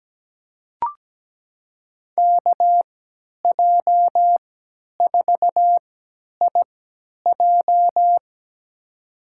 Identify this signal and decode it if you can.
Morse code, heard as a keyed tone, about 17 wpm, spaced as for 10 wpm, 700 Hz. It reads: KJ4IJ
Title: USA Call Signs - 5 Characters - 3x Character Spacing - 17wpm